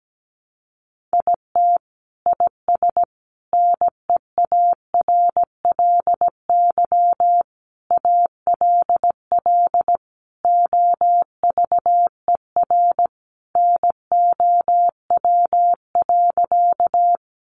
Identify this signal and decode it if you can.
Morse code, heard as a keyed tone, about 17 wpm, 700 Hz. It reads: IT IS NEARLY ALL OVER NOW.
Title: Sentences from Top 1300 Words 17wpm